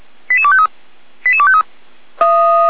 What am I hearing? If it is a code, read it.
Five-tone selective call (ZVEI), heard as quick runs of five tones, tones of about 80 ms. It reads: 88253 88253
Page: /455/SDR/ZVEI/